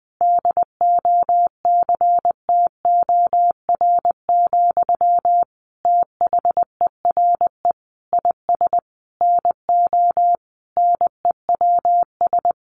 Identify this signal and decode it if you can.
Morse code, heard as a keyed tone, about 19 wpm, 700 Hz. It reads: DOCTOR, THERE IS NO NEWS
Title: Sentences from Top 1500 Words 20wpm